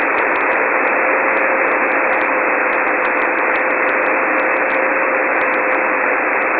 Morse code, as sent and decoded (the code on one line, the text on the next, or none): none